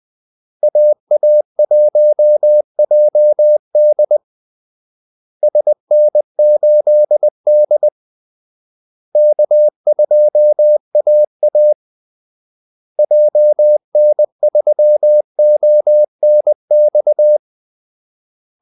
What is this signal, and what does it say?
Morse code, heard as a keyed tone, about 20 wpm, 600 Hz. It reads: AA1JD SN8D K2AA JN3ONX
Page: Kaldesignaler 20 wpm